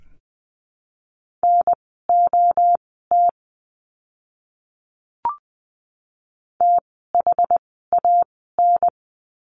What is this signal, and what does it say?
Morse code, heard as a keyed tone, about 20 wpm, spaced as for 15 wpm, 700 Hz. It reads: NOT THAN
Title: 061 Lesson 11f - Letter h - Words 20wpm